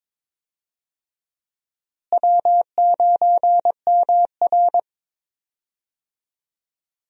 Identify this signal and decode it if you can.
Morse code, heard as a keyed tone, about 22 wpm, 700 Hz. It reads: W9MR
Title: USA Call Signs - 4 Characters - Repeated 3x - 22wpm